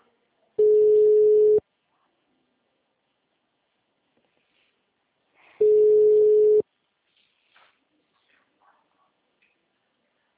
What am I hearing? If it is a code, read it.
Ringback tone (Europe, 425 Hz).